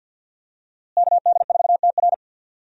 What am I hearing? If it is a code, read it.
Morse code, heard as a keyed tone, about 52 wpm, 700 Hz. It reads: KD4TR
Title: USA Call Signs - 5 Characters - Repeated 3x - 50wpm